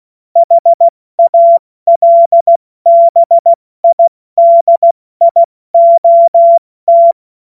Morse code, heard as a keyed tone, about 16 wpm, spaced as for 14 wpm, 680 Hz. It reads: HALBIDIOT
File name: Morsekurs / Witze / Witz 13. September 2022 70 F80